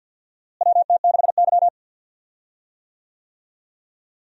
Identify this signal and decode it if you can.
Morse code, heard as a keyed tone, about 50 wpm, 700 Hz. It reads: WT6Y
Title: USA Call Signs - 4 Characters - Repeated 2x - 50wpm